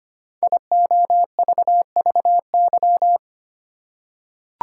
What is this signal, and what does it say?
Morse code, heard as a keyed tone, about 25 wpm, 700 Hz. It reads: IOVVY
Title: Code Groups - 5 Letter:Numbers - 25wpm